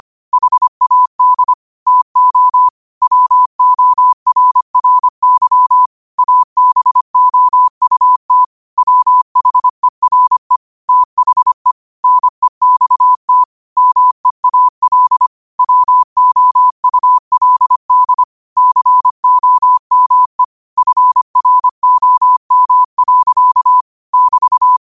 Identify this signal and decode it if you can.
Morse code, heard as a keyed tone, about 25 wpm, 1000 Hz. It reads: HAD TO WORRY ABOUT WHERE THE NEXT MEAL WOULD COME FROM. <BT>
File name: qotdfiles / 25wpm